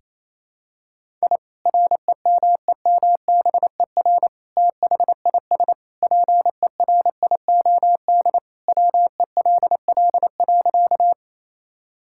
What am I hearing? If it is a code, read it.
Morse code, heard as a keyed tone, about 28 wpm, 700 Hz. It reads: I REMEMBER THIS PERIOD WELL.
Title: Sentences from Top 1000 Words - Review Part 5 - 28wpm